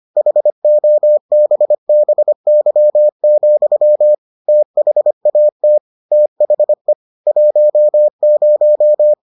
Morse code, heard as a keyed tone, about 25 wpm, 600 Hz. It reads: HOBBY, THAT THE 10